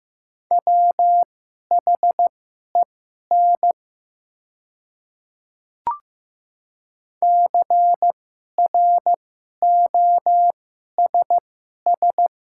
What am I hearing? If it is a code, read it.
Morse code, heard as a keyed tone, about 15 wpm, spaced as for 11 wpm, 690 Hz. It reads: WHEN CROSS
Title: 2x Character Spacing - Top 500 Words - 15wpm